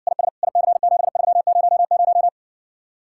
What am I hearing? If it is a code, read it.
Morse code, heard as a keyed tone, about 60 wpm, 700 Hz. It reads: IS IC7300